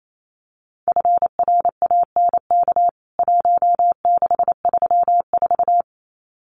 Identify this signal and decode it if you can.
Morse code, heard as a keyed tone, about 28 wpm, 700 Hz. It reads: FRANK 1634
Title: CWT Exchange 28wpm